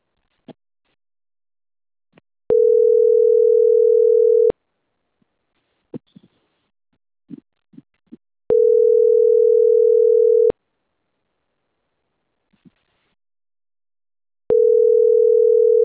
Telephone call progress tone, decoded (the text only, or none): ringback tone